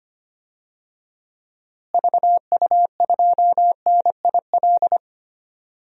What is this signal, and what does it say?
Morse code, heard as a keyed tone, about 25 wpm, 700 Hz. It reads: VU2NIL
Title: DXCC-100 Call Signs - 6 Characters - Repeated 2x - 25wpm